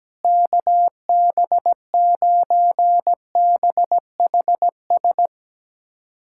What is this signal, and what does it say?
Morse code, heard as a keyed tone, about 17 wpm, 700 Hz. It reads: KB9BHS
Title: USA Callsign DE Callsign 17wpm